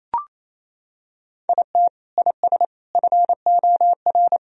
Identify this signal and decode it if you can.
Morse code, heard as a keyed tone, about 28 wpm, 700 Hz. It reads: IT IS FOR
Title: Sentences from Top 1000 Words - Review Part 4 - 28wpm